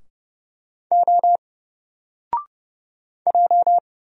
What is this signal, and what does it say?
Morse code, heard as a keyed tone, about 30 wpm, 700 Hz. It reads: O J
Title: Single Letter-Number Alt 30wpm